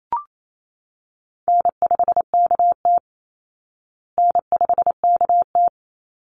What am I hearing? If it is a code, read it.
Morse code, heard as a keyed tone, about 28 wpm, 700 Hz. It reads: N5KT N5KT
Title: Call Signs Repeated 3x - 28wpm